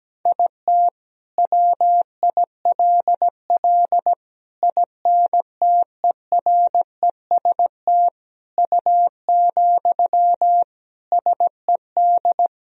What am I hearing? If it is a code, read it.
Morse code, heard as a keyed tone, about 17 wpm, 700 Hz. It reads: IT WILL INTEREST U, SED
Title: Sentences from Top 400 Words and Abbreviations 17wpm